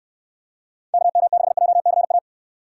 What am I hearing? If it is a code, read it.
Morse code, heard as a keyed tone, about 63 wpm, 700 Hz. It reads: KM6JCA